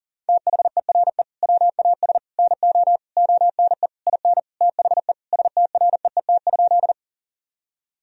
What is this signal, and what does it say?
Morse code, heard as a keyed tone, about 37 wpm, 700 Hz. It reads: THERE WAS NO ONE IN THE STREET?
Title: Sentences from Top 1000 Words - Review Part 3 - 40wpm